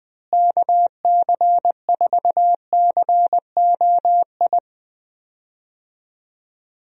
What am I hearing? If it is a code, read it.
Morse code, heard as a keyed tone, about 20 wpm, 700 Hz. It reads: KC4COI
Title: USA Call Signs - 6 Characters - Repeated 2x - 20wpm